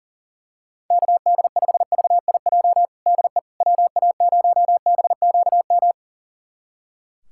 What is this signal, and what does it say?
Morse code, heard as a keyed tone, about 37 wpm, 700 Hz. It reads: KD5VIJ DE WA0BQM
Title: USA Callsign DE Callsign - 6 Characters - 40wpm